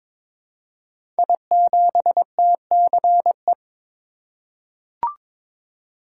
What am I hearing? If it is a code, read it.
Morse code, heard as a keyed tone, about 23 wpm, 700 Hz. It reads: I7TCE
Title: Code Groups - 5 Letter:Numbers - 22wpm